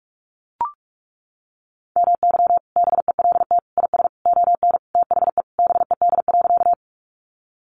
Morse code, heard as a keyed tone, about 44 wpm, 700 Hz. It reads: MY BELT IS ON THE BED.
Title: Sentences from Top 1400 Words 45wpm